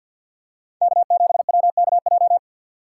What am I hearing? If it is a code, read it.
Morse code, heard as a keyed tone, about 53 wpm, 700 Hz. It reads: K7WKJ
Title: USA Call Signs - 6x Character Spacing - 50wpm